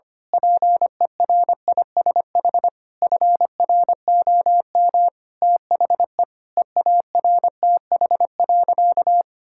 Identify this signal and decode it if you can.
Morse code, heard as a keyed tone, about 25 wpm, 700 Hz. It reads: PERISH FROM THE EARTH.
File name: practice / 25wpm / Sentences with 9-Word Phrases - 25wpm